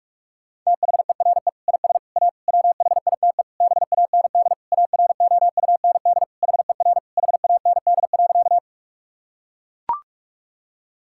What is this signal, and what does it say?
Morse code, heard as a keyed tone, about 41 wpm, 700 Hz. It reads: THERE IS A WHITE BAND AROUND HER HAND.
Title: Sentences from Top 1600 Words 45wpm